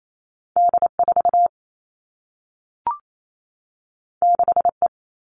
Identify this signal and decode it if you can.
Morse code, heard as a keyed tone, about 28 wpm, 700 Hz. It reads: D4 6E
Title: All Cut Numbers - 2-Digit 28wpm